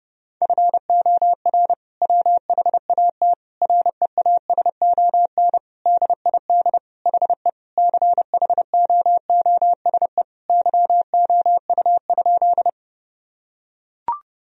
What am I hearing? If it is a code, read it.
Morse code, heard as a keyed tone, about 30 wpm, 700 Hz. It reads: FOR WHAT REASON DID HE CHOOSE YOU?
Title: Sentences from Top 1000 Words - Review Part 4 - 30wpm